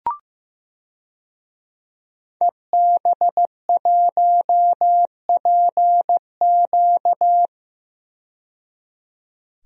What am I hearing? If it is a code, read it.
Morse code, heard as a keyed tone, about 15 wpm, 700 Hz. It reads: EB1PQ
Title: DXCC-100 Call Signs - 5 Characters - 15wpm